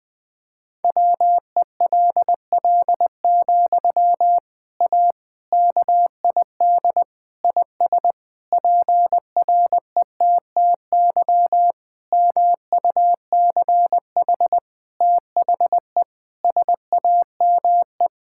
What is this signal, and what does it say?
Morse code, heard as a keyed tone, about 20 wpm, 700 Hz. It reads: WELL, A KID IS PRETTY MUCH THE SAME
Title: Sentences from Top 1600 Words 20wpm